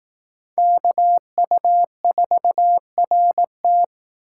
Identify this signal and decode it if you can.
Morse code, heard as a keyed tone, about 18 wpm, 700 Hz. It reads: KU4RT
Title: Speed Racing - USA 2x2 Call Signs 15wpm